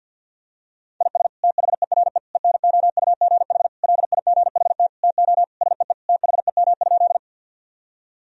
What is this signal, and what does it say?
Morse code, heard as a keyed tone, about 52 wpm, 700 Hz. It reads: IS THERE ENOUGH LIGHT TO SEE THEM?
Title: Sentences from Top 1000 Words - Review Part 1 - 50wpm